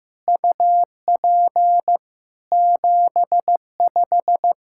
Morse code, heard as a keyed tone, about 15 wpm, 700 Hz. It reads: UP 75
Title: Practice QSO - Exchange 2 - Info - 15wpm